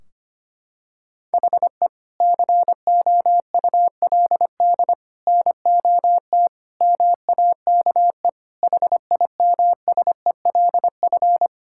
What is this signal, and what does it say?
Morse code, heard as a keyed tone, about 25 wpm, 700 Hz. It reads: HE COULD NOT MAKE HIMSELF
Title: Sentences from Top 1600 Words 25wpm